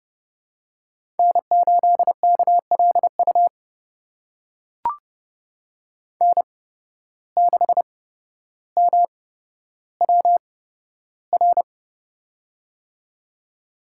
Morse code, heard as a keyed tone, about 30 wpm, spaced as for 13 wpm, 700 Hz. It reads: N8KLU N6MWR
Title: USA Call Signs - 5 Characters - 8x Character Spacing - 30wpm